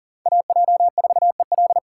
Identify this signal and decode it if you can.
Morse code, heard as a keyed tone, about 40 wpm, 700 Hz. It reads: AJ4EL
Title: USA Call Signs - 5 Characters - 4x Character Spacing - 40wpm